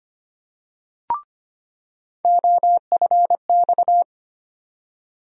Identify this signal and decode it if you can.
Morse code, heard as a keyed tone, about 25 wpm, 700 Hz. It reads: OFX
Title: USA Call Sign Postfixes - 3 Letters 25wpm